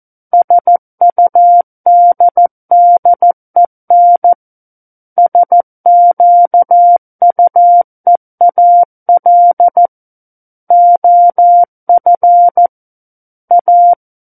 Morse code, heard as a keyed tone, about 14 wpm, 700 Hz. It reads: SUDDEN SQUEAL OF A
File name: Book One / 05-Chapter 5 - 14 WPM